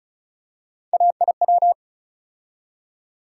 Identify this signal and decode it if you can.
Morse code, heard as a keyed tone, about 34 wpm, 700 Hz. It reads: AIW
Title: USA Call Sign Postfixes - 3 Letters 35wpm